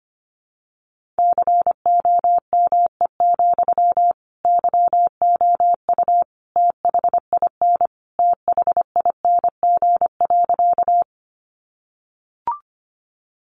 Morse code, heard as a keyed tone, about 25 wpm, 700 Hz. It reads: COME, YOU THIN THING.
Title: Sentences from Top 1200 Words 25wpm